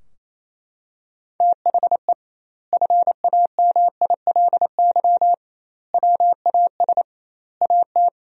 Morse code, heard as a keyed tone, about 28 wpm, 700 Hz. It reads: THE FAMILY WAS AT
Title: 2x Word-Spacing - Sentences from Top 400 Words - 28wpm